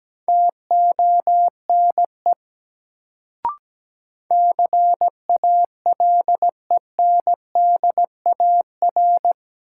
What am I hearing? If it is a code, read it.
Morse code, heard as a keyed tone, about 17 wpm, 700 Hz. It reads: TONE CALENDAR
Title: Sentences from Top 1600 Words - New Word Drill Alt - 17wpm